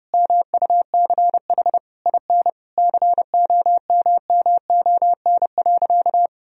Morse code, heard as a keyed tone, about 30 wpm, 700 Hz. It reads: MUCH IN COMMON.
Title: Sentences from Top 1000 Words - Review Part 3 - 30wpm